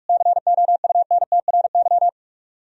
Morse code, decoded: COUNTRY

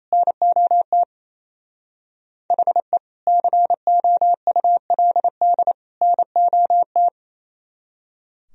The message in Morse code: NOT HE COULD NOT

